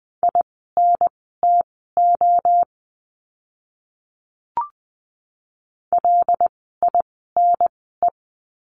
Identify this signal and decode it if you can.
Morse code, heard as a keyed tone, about 20 wpm, spaced as for 15 wpm, 700 Hz. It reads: INTO LINE
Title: 115 Lesson 19f - Letter w - Words 20wpm